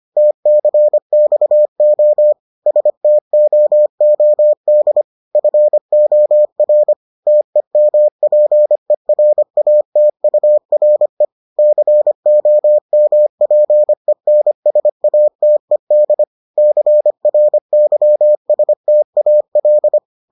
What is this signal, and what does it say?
Morse code, heard as a keyed tone, about 25 wpm, 600 Hz. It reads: TCXO STOOD FOR TEMPERATURE COMPENSATED CRYSTAL